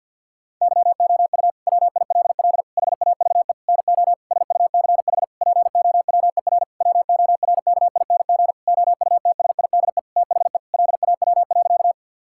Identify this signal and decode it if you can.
Morse code, heard as a keyed tone, about 50 wpm, 700 Hz. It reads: YOU WILL HAVE NO SUCH POWER WORKING OUTSIDE THE LAW.
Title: Sentences from Top 1000 Words - Review Part 6 - 50wpm